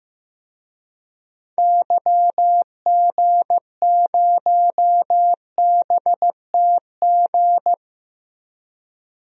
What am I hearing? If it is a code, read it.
Morse code, heard as a keyed tone, about 15 wpm, 700 Hz. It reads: YG0BTG
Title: DXCC-100 Call Signs - 6 Characters - 15wpm